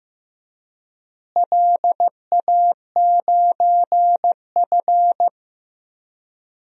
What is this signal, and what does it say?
Morse code, heard as a keyed tone, about 15 wpm, 700 Hz. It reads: LA9F
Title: DXCC-100 Call Signs - 4 Characters - Repeated 2x - 15wpm